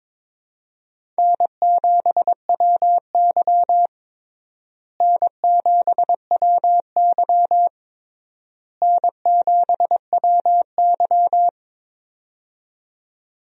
Morse code, heard as a keyed tone, about 22 wpm, 700 Hz. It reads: N7WY N7WY N7WY
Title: USA Call Signs - 4 Characters - Repeated 3x - 22wpm